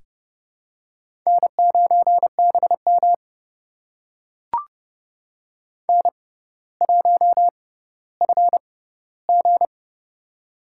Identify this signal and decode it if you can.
Morse code, heard as a keyed tone, about 30 wpm, spaced as for 15 wpm, 700 Hz. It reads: N9BM N1FG
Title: USA Call Signs - 4 Characters - 6x Character Spacing - 30wpm